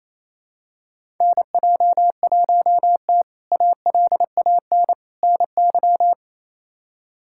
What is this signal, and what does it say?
Morse code, heard as a keyed tone, about 28 wpm, 700 Hz. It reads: NJ1T ALAN NY
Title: K1USN SST QSO Practice - 28wpm